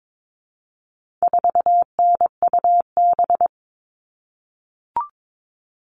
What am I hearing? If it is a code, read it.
Morse code, heard as a keyed tone, about 22 wpm, 700 Hz. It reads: 4NUB
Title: All Cut Numbers - 4-Digit 22wpm